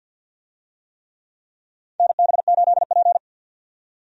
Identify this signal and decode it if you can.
Morse code, heard as a keyed tone, about 50 wpm, 700 Hz. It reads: NB8P
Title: USA Call Signs - 4 Characters - 50wpm